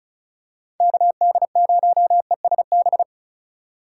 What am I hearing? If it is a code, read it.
Morse code, heard as a keyed tone, about 34 wpm, 700 Hz. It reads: KD0ESB
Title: USA Call Signs - 2x Character Spacing - 35wpm